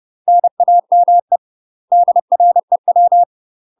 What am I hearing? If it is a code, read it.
Morse code, heard as a keyed tone, about 30 wpm, 700 Hz. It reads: NAME DREW
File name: UK MASTER QSOs TRAINING / 30WPM / UK MASTER QSOs 0035